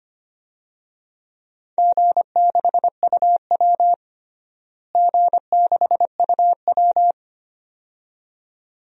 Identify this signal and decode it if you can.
Morse code, heard as a keyed tone, about 25 wpm, 700 Hz. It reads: G6UW G6UW
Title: DXCC-100 Call Signs - 4 Characters - Repeated 2x - 25wpm